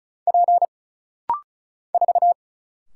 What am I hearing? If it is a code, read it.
Morse code, heard as a keyed tone, about 36 wpm, 700 Hz. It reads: P 4
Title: Single Letter-Number Alt - Rapid-Fire 35wpm